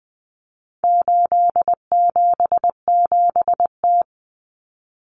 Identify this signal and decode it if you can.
Morse code, heard as a keyed tone, about 20 wpm, 700 Hz. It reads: 877T
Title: Common Cut Numbers - 4-Digit 20wpm